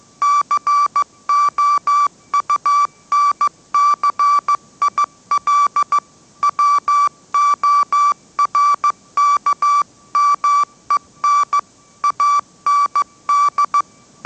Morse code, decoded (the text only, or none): COUNCIL WORKMEN AND